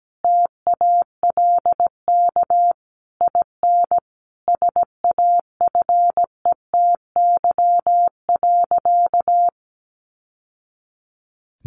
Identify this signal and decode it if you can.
Morse code, heard as a keyed tone, about 17 wpm, 700 Hz. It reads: TALK IN SAFETY.
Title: Sentences from Top 700 Words 17wpm